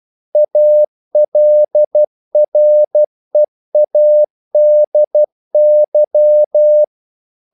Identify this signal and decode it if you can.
Morse code, heard as a keyed tone, about 12 wpm, 600 Hz. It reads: ALREADY
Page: English quotations in CW (telegraphy - morse)